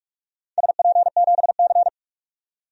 Morse code, heard as a keyed tone, about 44 wpm, 700 Hz. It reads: SP7C
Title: DXCC-100 Call Signs - 4 Characters - 45wpm